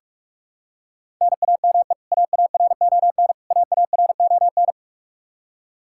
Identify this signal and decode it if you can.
Morse code, heard as a keyed tone, about 46 wpm, 700 Hz. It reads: NAME AARON AARON
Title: Practice QSO - Exchange 1 - Beginning - 45wpm